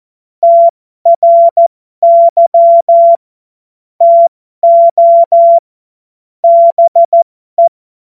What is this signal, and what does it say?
Morse code, heard as a keyed tone, about 14 wpm, spaced as for 12 wpm, 680 Hz. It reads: TRY TO BE
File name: Morsekurs / Witze